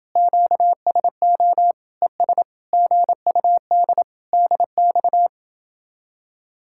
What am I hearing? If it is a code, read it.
Morse code, heard as a keyed tone, about 28 wpm, 700 Hz. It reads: QSO ES GUD DX